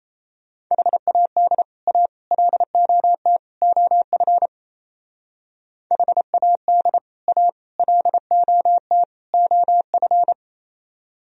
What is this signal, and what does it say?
Morse code, heard as a keyed tone, about 33 wpm, 700 Hz. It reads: HAD A LOT OF HAD A LOT OF
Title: Speed-Racing - Sets of 4 Words 28wpm